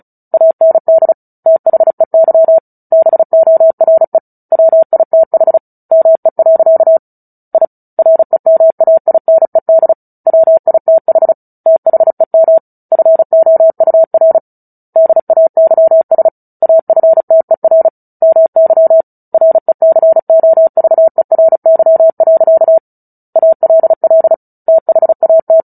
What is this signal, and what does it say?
Morse code, heard as a keyed tone, about 34 wpm, 650 Hz. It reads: AND THEY BORE WITH ME. I REMAINED WITH THEM FOUR DAYS AFTER MY RECOVERY. ALL THAT
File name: Book Two / 09-Chapter 9 - 35 WPM